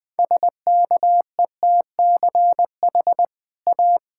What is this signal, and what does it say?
Morse code, heard as a keyed tone, about 20 wpm, 700 Hz. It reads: SKETCH A
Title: Easy Sentences with Hints - 20wpm